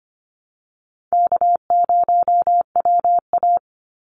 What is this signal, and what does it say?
Morse code, heard as a keyed tone, about 25 wpm, 700 Hz. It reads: K0WA
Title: Easy Call Signs Repeated 3x - 25wpm